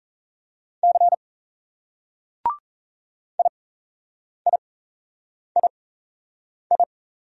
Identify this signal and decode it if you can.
Morse code, heard as a keyed tone, about 38 wpm, spaced as for 8 wpm, 700 Hz. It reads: C IIII